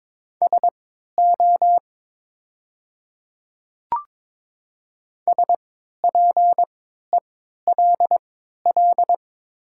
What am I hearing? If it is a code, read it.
Morse code, heard as a keyed tone, about 22 wpm, spaced as for 13 wpm, 700 Hz. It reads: SO SPELL
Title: 3x Character Spacing - Top 500 Words - 22wpm